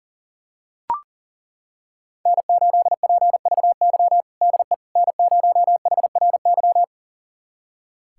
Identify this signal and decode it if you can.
Morse code, heard as a keyed tone, about 40 wpm, 700 Hz. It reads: N8PVY DE N0HRY
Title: USA Callsign DE Callsign - 5 Characters - 40wpm